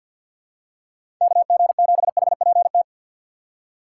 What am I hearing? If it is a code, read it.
Morse code, heard as a keyed tone, about 48 wpm, 700 Hz. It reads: KG7HPT
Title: USA Call Signs - 6x Character Spacing - 50wpm